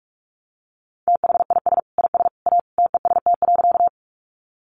Morse code, heard as a keyed tone, about 44 wpm, 700 Hz. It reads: THIS IS A TEST.
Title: Sentences from Top 500 Words 45wpm